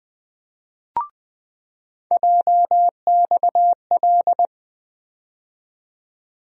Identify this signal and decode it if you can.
Morse code, heard as a keyed tone, about 20 wpm, 700 Hz. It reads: JXL
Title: USA Call Sign Postfixes - 3 Letters 20wpm